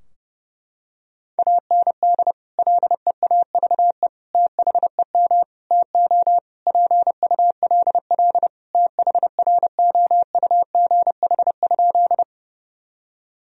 Morse code, heard as a keyed tone, about 30 wpm, 700 Hz. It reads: AND LEAVE THEM TO PULL THROUGH?
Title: Sentences from Top 1000 Words - Review Part 2 - 30wpm